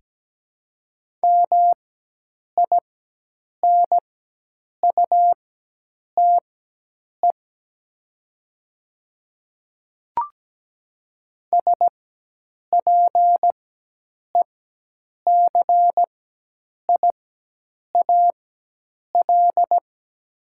Morse code, heard as a keyed tone, about 17 wpm, spaced as for 8 wpm, 700 Hz. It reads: MINUTE SPECIAL
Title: 4x Character Spacing - Top 500 Words - 17wpm